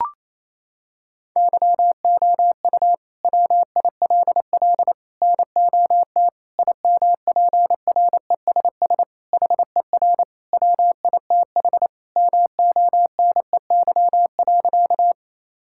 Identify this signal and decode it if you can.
Morse code, heard as a keyed tone, about 28 wpm, 700 Hz. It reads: YOU WILL NOT IMPRESS HER WITH MONEY.
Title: Sentences from Top 1600 Words 28wpm